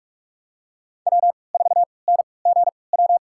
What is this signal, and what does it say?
Morse code, heard as a keyed tone, about 46 wpm, spaced as for 26 wpm, 700 Hz. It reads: W4NGW
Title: USA Call Signs - 5 Characters - 3x Character Spacing - 45wpm